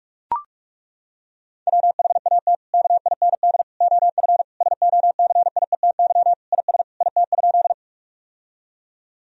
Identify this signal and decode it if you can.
Morse code, heard as a keyed tone, about 41 wpm, 700 Hz. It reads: WHAT KIND OF SOCIETY IS IT?
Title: Sentences from Top 1000 Words - Review Part 4 - 45wpm